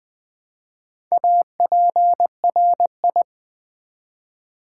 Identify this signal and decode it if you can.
Morse code, heard as a keyed tone, about 20 wpm, 700 Hz. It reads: APRI
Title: Build A Word - 1 Letter At A Time - 3 to 5-Letter Words - 20wpm